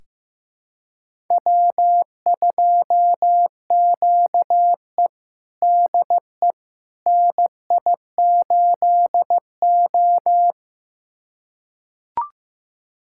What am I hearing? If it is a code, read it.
Morse code, heard as a keyed tone, about 16 wpm, 700 Hz. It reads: W2QE DE NI8O
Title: USA Callsign DE Callsign - 4 Characters - 15wpm